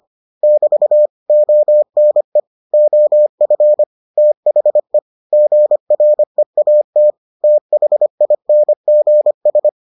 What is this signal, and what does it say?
Morse code, heard as a keyed tone, about 25 wpm, 600 Hz. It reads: <BT> ONE OF THE GREAT THINGS